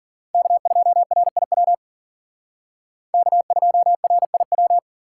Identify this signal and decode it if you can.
Morse code, heard as a keyed tone, about 51 wpm, 700 Hz. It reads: K2RIW K2RIW